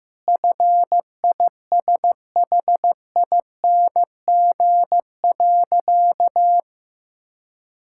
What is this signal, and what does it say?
Morse code, heard as a keyed tone, about 15 wpm, 700 Hz. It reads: FISHING.
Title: Sentences from Top 1000 Words - Review Part 5 - 15wpm